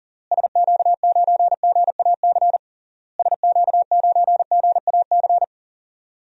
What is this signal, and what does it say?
Morse code, heard as a keyed tone, about 40 wpm, 700 Hz. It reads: SQ9GAC SQ9GAC
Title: DXCC-100 Call Signs - 6 Characters - Repeated 3x - 40wpm